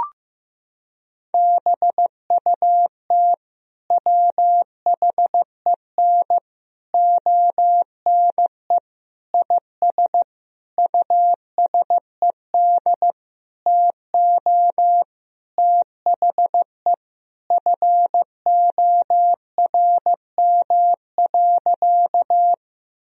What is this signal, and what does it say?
Morse code, heard as a keyed tone, about 15 wpm, 700 Hz. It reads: BUT WHEN ONE IS USED TO THE FORM.
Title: Sentences from Top 1000 Words - Review Part 1 - 15wpm